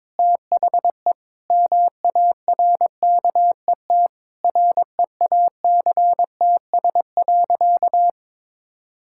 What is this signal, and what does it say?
Morse code, heard as a keyed tone, about 22 wpm, 700 Hz. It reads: THE MARKET REACTS.
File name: practice / 22wpm / Sentences with 8-Word Phrases - 22wpm